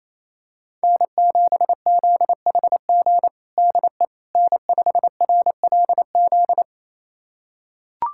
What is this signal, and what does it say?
Morse code, heard as a keyed tone, about 28 wpm, 700 Hz. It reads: N7ZHG DE N5RLZ
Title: USA Callsign DE Callsign - 5 Characters - 28wpm